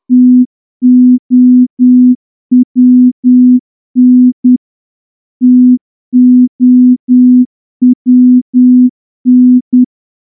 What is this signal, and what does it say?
Morse code, heard as a keyed tone, about 10 wpm, 250 Hz. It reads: TOWN TOWN
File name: morse-training-10WPM